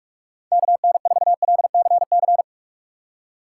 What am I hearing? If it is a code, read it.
Morse code, heard as a keyed tone, about 44 wpm, 700 Hz. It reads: KN4LCC
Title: USA Call Signs - 6 Characters - 2x Character Spacing - 45wpm